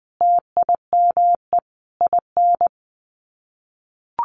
Morse code, heard as a keyed tone, about 20 wpm, 700 Hz. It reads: TIME IN
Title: Sets of 4 Words 20wpm